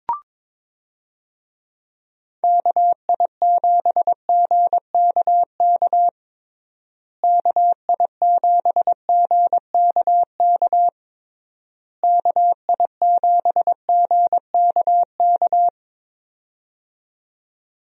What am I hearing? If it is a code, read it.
Morse code, heard as a keyed tone, about 21 wpm, 700 Hz. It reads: KI7GKK KI7GKK KI7GKK
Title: USA Call Signs - 6 Characters - Repeated 3x - 22wpm